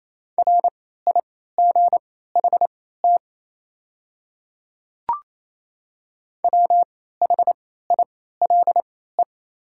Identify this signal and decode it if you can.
Morse code, heard as a keyed tone, about 28 wpm, spaced as for 16 wpm, 700 Hz. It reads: RIGHT WHILE
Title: 3x Character Spacing - Top 300 Words - 28wpm